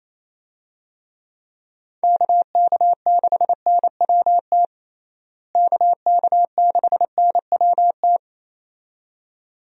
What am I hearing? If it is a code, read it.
Morse code, heard as a keyed tone, about 28 wpm, 700 Hz. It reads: KK6NWT KK6NWT
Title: USA Call Signs - 6 Characters - Repeated 2x - 28wpm